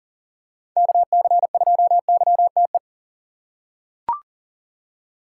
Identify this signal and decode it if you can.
Morse code, heard as a keyed tone, about 40 wpm, 700 Hz. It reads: KC2YTE
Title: USA Call Signs - 3x Character Spacing - 40wpm